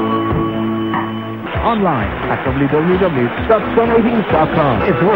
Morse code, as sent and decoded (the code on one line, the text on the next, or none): none